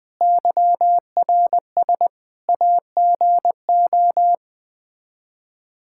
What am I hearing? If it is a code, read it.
Morse code, heard as a keyed tone, about 21 wpm, 700 Hz. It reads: YRS AGO
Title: Practice QSO - Exchange 3 - More Info - 20wpm